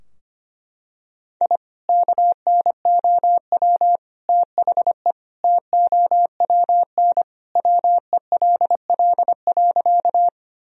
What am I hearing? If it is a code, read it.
Morse code, heard as a keyed tone, about 25 wpm, 700 Hz. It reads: I KNOW THE TOWN WELL.
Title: Sentences from Top 1000 Words - Review Part 2 - 25wpm